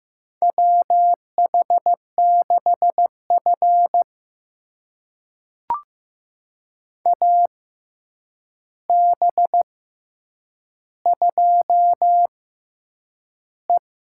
Morse code, heard as a keyed tone, about 15 wpm, spaced as for 8 wpm, 700 Hz. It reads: WH6F AB2E
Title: USA Call Signs - 4 Characters - 6x Character Spacing - 15wpm